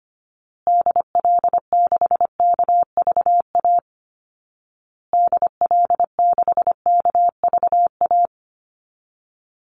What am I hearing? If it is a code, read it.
Morse code, heard as a keyed tone, about 25 wpm, 700 Hz. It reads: DL6KVA DL6KVA
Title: Call Signs Repeated 2x - 25wpm